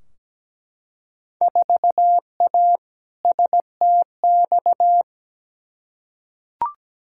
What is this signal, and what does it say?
Morse code, heard as a keyed tone, about 17 wpm, 700 Hz. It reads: 4A STX
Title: Field Day Exchanges - 17wpm